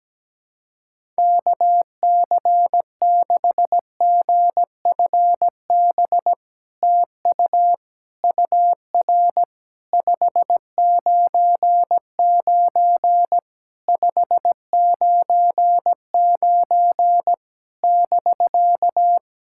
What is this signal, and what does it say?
Morse code, heard as a keyed tone, about 17 wpm, 700 Hz. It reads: KC6GFB TU UR 599 599 <BK>
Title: SOTA QSO Practice 17wpm